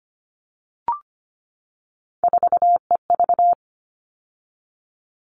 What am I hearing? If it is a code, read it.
Morse code, heard as a keyed tone, about 25 wpm, 700 Hz. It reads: 4EV
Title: All Cut Numbers - 3-Digit 25wpm